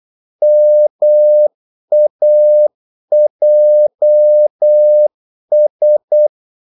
Morse code, heard as a keyed tone, about 8 wpm, 600 Hz. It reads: MAJS
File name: Korte ord DK 8wpm